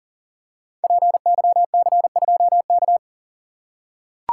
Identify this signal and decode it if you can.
Morse code, heard as a keyed tone, about 39 wpm, 700 Hz. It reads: PYC2K
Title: Code Groups - 5 Letter:Numbers - 40wpm